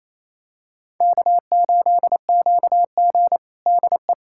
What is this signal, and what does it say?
Morse code, heard as a keyed tone, about 28 wpm, 700 Hz. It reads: K8QG DE
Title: USA Callsign DE Callsign - 4 Characters - 28wpm